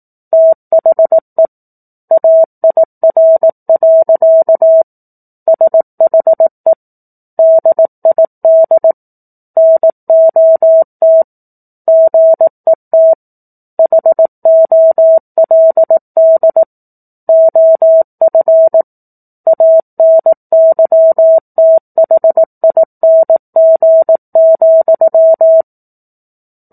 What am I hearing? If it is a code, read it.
Morse code, heard as a keyed tone, about 18 wpm, 650 Hz. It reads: THE AIR. SHE DID NOT GET HOLD OF ANYTHING,
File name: 04-Chapter-04 18 WPM